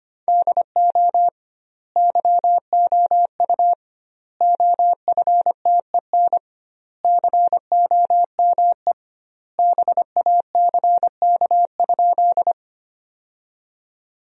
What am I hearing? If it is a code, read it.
Morse code, heard as a keyed tone, about 25 wpm, 700 Hz. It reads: DO YOU OFTEN COME BACK?
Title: 2x Word-Spacing - Sentences from Top 300 Words - 25wpm